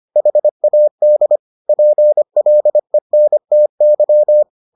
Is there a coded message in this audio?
Yes: Morse code from 0.2 to 4.4 s, heard as a keyed tone, about 25 wpm, 600 Hz. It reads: HAD PLENTY